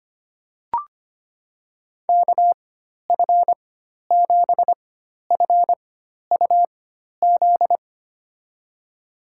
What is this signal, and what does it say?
Morse code, heard as a keyed tone, about 25 wpm, spaced as for 12 wpm, 700 Hz. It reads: KF7FUZ
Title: USA Call Signs - 6 Characters - 4x Character Spacing - 25wpm